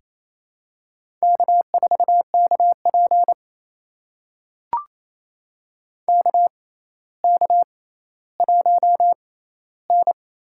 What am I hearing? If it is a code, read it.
Morse code, heard as a keyed tone, about 28 wpm, spaced as for 14 wpm, 700 Hz. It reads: K4KP KK1N
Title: USA Call Signs - 4 Characters - 6x Character Spacing - 28wpm